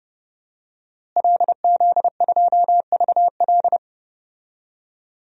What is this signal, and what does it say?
Morse code, heard as a keyed tone, about 30 wpm, 700 Hz. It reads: LZ2VL
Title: DXCC-100 Call Signs - 5 Characters - Repeated 3x - 30wpm